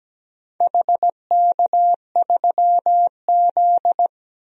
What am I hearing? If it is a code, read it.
Morse code, heard as a keyed tone, about 17 wpm, 700 Hz. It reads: HK3Z
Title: DXCC-100 Call Signs - 4 Characters - Repeated 3x - 17wpm